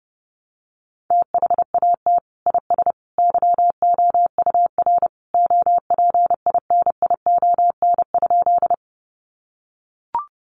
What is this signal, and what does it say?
Morse code, heard as a keyed tone, about 30 wpm, 700 Hz. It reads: THAT IS YOUR OPINION?
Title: Sentences from Top 1200 Words 30wpm